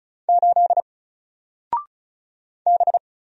Morse code, heard as a keyed tone, about 36 wpm, 700 Hz. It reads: 8 B
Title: Single Letter-Number Alt 35wpm